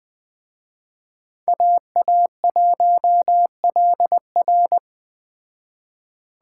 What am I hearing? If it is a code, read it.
Morse code, heard as a keyed tone, about 20 wpm, 700 Hz. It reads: AA1LR
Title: Speed Racing - USA 2x2 Call Signs 17wpm